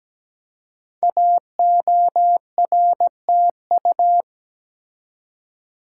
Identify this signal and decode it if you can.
Morse code, heard as a keyed tone, about 17 wpm, 700 Hz. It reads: AORTU
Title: Code Groups - 5 Letters - 17wpm